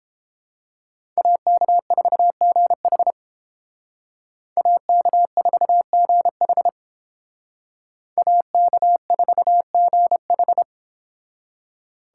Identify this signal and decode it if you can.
Morse code, heard as a keyed tone, about 31 wpm, 700 Hz. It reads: AK4GH AK4GH AK4GH